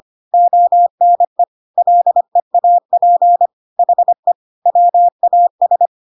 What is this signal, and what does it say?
Morse code, heard as a keyed tone, about 25 wpm, 700 Hz. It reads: ONE LEAP HE WAS
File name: Pinocchio / Pinocchio ch3 25 25 ik1hge